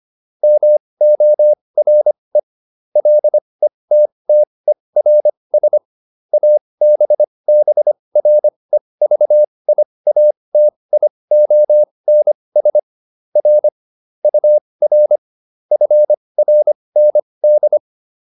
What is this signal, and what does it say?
Morse code, heard as a keyed tone, about 25 wpm, spaced as for 20 wpm, 600 Hz. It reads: MORE LETTERS ABBREVIATIONS R UR FRND